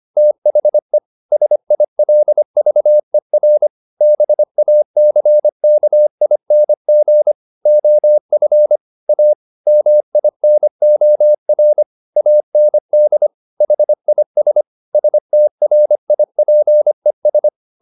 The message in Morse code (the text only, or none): THE SILVER BACKING OF A MINOR AND HIS STRIPES